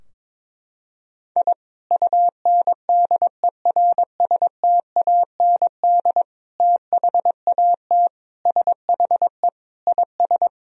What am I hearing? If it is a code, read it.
Morse code, heard as a keyed tone, about 22 wpm, 700 Hz. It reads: I UNDERSTAND THAT SHE IS
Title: Sentences from Top 1000 Words - Review Part 3 - 22wpm